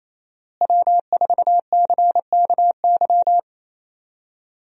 Morse code, heard as a keyed tone, about 28 wpm, 700 Hz. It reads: W4CKY
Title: USA Call Signs - 5 Characters - Repeated 3x - 28wpm